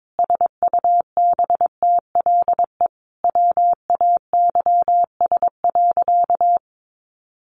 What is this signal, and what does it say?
Morse code, heard as a keyed tone, about 22 wpm, 700 Hz. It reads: SUBTLE WAYS.
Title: A Sound Of Thunder 22wpm